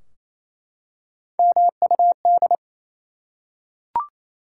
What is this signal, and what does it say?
Morse code, heard as a keyed tone, about 27 wpm, 700 Hz. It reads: MUD
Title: Three Letter Words - 28wpm